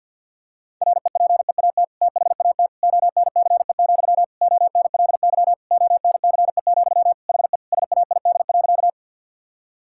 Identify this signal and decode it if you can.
Morse code, heard as a keyed tone, about 50 wpm, 700 Hz. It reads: REPEAT THAT ONCE, ONLY ONCE, HE SAID.
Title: Sentences from Top 1600 Words 50wpm